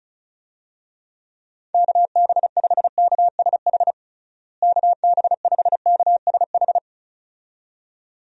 Morse code, heard as a keyed tone, about 34 wpm, 700 Hz. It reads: KB5KSH KB5KSH
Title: USA Call Signs - 6 Characters - Repeated 2x - 35wpm